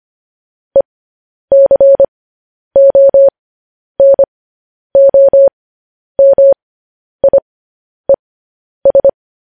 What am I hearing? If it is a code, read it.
Morse code, heard as a keyed tone, about 25 wpm, spaced as for 10 wpm, 550 Hz. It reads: ECONOMIES
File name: podcast / News headlines-1x-25x10